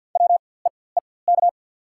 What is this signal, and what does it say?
Morse code, heard as a keyed tone, about 47 wpm, spaced as for 23 wpm, 690 Hz. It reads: WEEK
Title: 4x Character Spacing - Top 500 Words - 50wpm